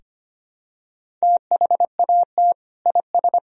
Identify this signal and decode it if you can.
Morse code, heard as a keyed tone, about 25 wpm, 700 Hz. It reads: THAT IS
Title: Sentences from Top 200 Words and Abbreviations 25wpm